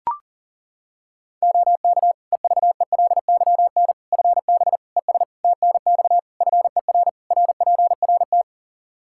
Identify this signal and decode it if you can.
Morse code, heard as a keyed tone, about 40 wpm, 700 Hz. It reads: OK EVELYN FB ES TNX FER RPRT
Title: Practice QSO - Exchange 2 - Info - 40wpm